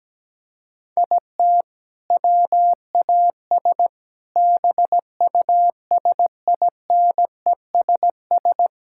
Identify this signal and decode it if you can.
Morse code, heard as a keyed tone, about 17 wpm, 700 Hz. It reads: IT WAS BUSINESS
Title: Sentences from Top 1000 Words - Review Part 5 - 17wpm